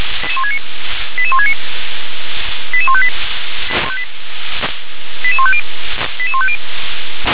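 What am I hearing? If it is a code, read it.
Five-tone selective call (ZVEI), heard as quick runs of five tones, tones of about 70 ms. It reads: 88169 88158 88150 88150
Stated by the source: Five-tone ZVEI selective call received on SDR.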